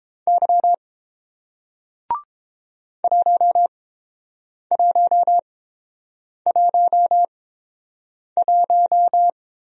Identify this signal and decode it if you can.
Morse code, heard as a keyed tone, about 30 wpm, spaced as for 7 wpm, 700 Hz. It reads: Y 1111